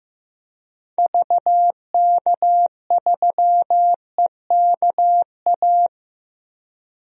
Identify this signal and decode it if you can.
Morse code, heard as a keyed tone, about 16 wpm, 700 Hz. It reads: VK3EKA
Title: DXCC-100 Call Signs - 6 Characters - Repeated 2x - 15wpm